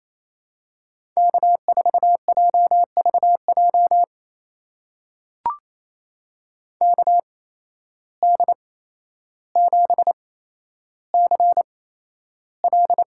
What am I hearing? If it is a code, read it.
Morse code, heard as a keyed tone, about 27 wpm, spaced as for 12 wpm, 700 Hz. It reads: K4JVJ KD7CL
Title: USA Call Signs - 5 Characters - 8x Character Spacing - 28wpm